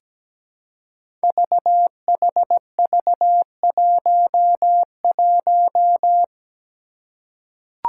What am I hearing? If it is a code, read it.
Morse code, heard as a keyed tone, about 17 wpm, 700 Hz. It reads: VHV11
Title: Five Challenging Letter-Number Combinations 17wpm